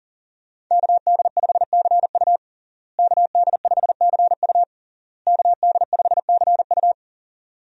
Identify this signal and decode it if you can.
Morse code, heard as a keyed tone, about 40 wpm, 700 Hz. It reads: KD5CU KD5CU KD5CU
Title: USA Call Signs - 5 Characters - Repeated 3x - 40wpm